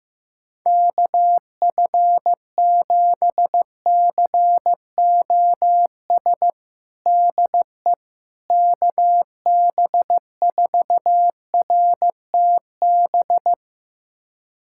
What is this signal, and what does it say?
Morse code, heard as a keyed tone, about 15 wpm, 700 Hz. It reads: KF7COS DE KB4RTB
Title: USA Callsign DE Callsign - 6 Characters - 15wpm